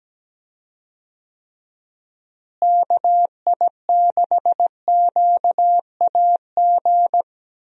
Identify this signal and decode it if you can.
Morse code, heard as a keyed tone, about 17 wpm, 700 Hz. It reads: KI6QAG
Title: USA Call Signs - 6 Characters - Repeated 3x - 17wpm